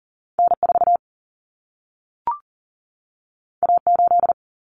Morse code, heard as a keyed tone, about 40 wpm, 700 Hz. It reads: N4 A8
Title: Common Cut Numbers - 2-Digit 40wpm